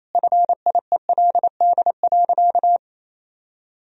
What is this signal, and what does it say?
Morse code, heard as a keyed tone, about 28 wpm, 700 Hz. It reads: FIELD.
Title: Sentences from Top 1000 Words - Review Part 2 - 28wpm